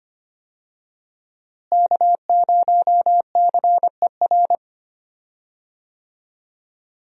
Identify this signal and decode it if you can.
Morse code, heard as a keyed tone, about 25 wpm, 700 Hz. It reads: K0CER
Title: USA Call Signs - 5 Characters - Repeated 2x - 25wpm